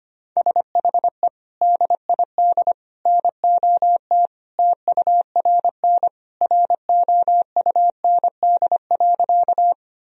Morse code, heard as a keyed tone, about 25 wpm, 700 Hz. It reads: SHE DID NOT TURN ROUND.
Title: Sentences from Top 1000 Words - Review Part 1 - 25wpm